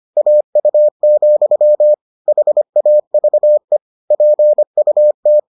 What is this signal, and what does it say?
Morse code, heard as a keyed tone, about 25 wpm, 600 Hz. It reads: AU, HAVE PUT